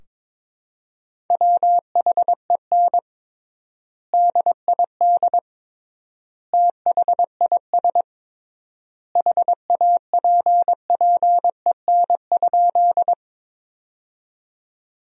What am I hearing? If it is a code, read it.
Morse code, heard as a keyed tone, about 22 wpm, 700 Hz. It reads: WHEN DID THIS HAPPEN?
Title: 3x Word-Spacing - Sentences from Top 400 Words - 22wpm